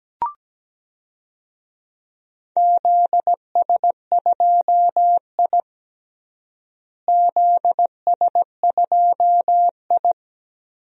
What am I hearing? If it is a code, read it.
Morse code, heard as a keyed tone, about 17 wpm, 700 Hz. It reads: ZS2I ZS2I
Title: DXCC-100 Call Signs - 4 Characters - Repeated 3x - 17wpm